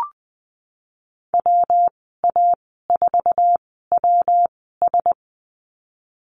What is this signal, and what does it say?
Morse code, heard as a keyed tone, about 20 wpm, spaced as for 14 wpm, 700 Hz. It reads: WA4WS
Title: 122 Lesson 20g - Number 3 - Call Signs 20wpm